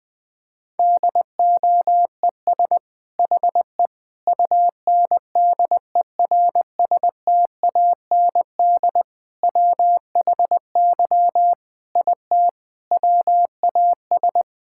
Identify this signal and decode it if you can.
Morse code, heard as a keyed tone, about 20 wpm, 700 Hz. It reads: DOES HE UNDERSTAND WHY IT WAS
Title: Sentences from Top 1000 Words - Review Part 5 - 20wpm